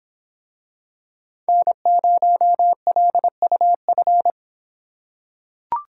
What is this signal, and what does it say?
Morse code, heard as a keyed tone, about 24 wpm, 700 Hz. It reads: N0LUF